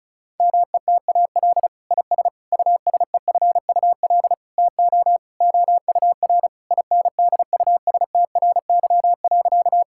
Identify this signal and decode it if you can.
Morse code, heard as a keyed tone, about 34 wpm, 700 Hz. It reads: METAL IS USEFUL TO OUR INDUSTRY.
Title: Sentences from Top 1000 Words - Review Part 6 - 35wpm